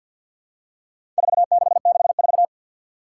Morse code, heard as a keyed tone, about 53 wpm, 700 Hz. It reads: 4664
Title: Four Challenging Letter-Number Combinations 50wpm